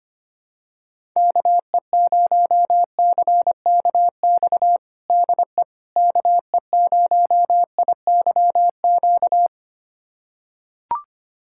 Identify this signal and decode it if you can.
Morse code, heard as a keyed tone, about 25 wpm, 700 Hz. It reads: KE0CKX DE KE0IYQ
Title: USA Callsign DE Callsign - 6 Characters - 25wpm